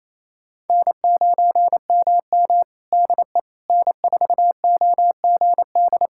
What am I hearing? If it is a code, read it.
Morse code, heard as a keyed tone, about 28 wpm, 700 Hz. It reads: N9MM DE N4OGD
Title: K1USN SST QSO Practice 28wpm